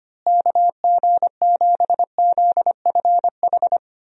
Morse code, heard as a keyed tone, about 25 wpm, 700 Hz. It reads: KG7ZFH
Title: USA Call Signs - 6 Characters - Repeated 2x - 25wpm